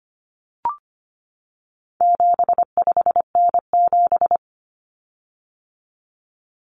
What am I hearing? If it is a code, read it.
Morse code, heard as a keyed tone, about 25 wpm, 700 Hz. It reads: 75N7
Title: Common Cut Numbers - 4-Digit 25wpm